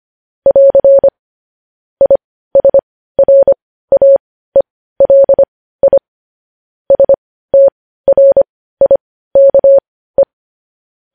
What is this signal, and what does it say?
Morse code, heard as a keyed tone, about 25 wpm, spaced as for 15 wpm, 550 Hz. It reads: <AR> ISRAELI STRIKE
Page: news-headlines-in-morse-code-at-15-wpm